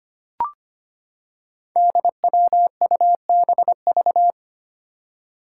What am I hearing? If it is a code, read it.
Morse code, heard as a keyed tone, about 25 wpm, 700 Hz. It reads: DWUBV
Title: Five Challenging Letter-Number Combinations 25wpm